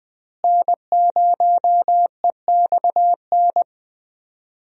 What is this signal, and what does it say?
Morse code, heard as a keyed tone, about 20 wpm, 700 Hz. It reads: N0EXN
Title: USA Call Signs - 5 Characters - Repeated 2x - 20wpm